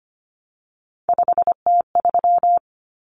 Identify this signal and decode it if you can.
Morse code, heard as a keyed tone, about 25 wpm, 700 Hz. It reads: 5T3
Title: Common Cut Numbers - 3-Digit 25wpm